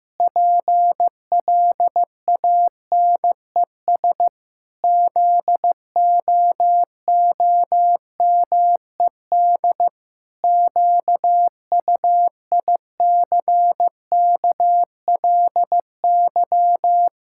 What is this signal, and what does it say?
Morse code, heard as a keyed tone, about 15 wpm, 700 Hz. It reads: PLANES ZOOMED QUICKLY
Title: Pangram Challenge v2 - 15wpm